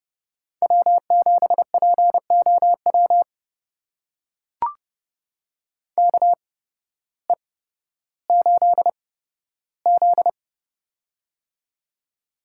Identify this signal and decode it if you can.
Morse code, heard as a keyed tone, about 31 wpm, spaced as for 14 wpm, 700 Hz. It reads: W7POW KE8Z
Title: USA Call Signs - 8x Character Spacing - 30wpm